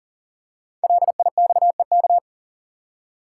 Morse code, heard as a keyed tone, about 43 wpm, 700 Hz. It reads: LIXEK
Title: Code Groups - 5 Letter:Numbers - 40wpm